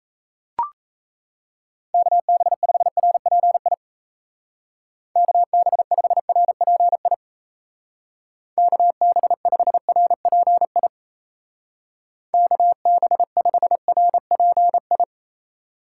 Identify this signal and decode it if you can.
Morse code, heard as a keyed tone, about 33 wpm, 700 Hz. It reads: KB5RPI KB5RPI KB5RPI KB5RPI
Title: Speed Racing - USA 2x3 Call Signs 28wpm